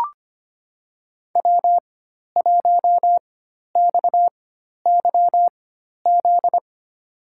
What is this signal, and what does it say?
Morse code, heard as a keyed tone, about 25 wpm, spaced as for 12 wpm, 700 Hz. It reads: W1XYZ
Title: USA Call Signs - 5 Characters - 4x Character Spacing - 25wpm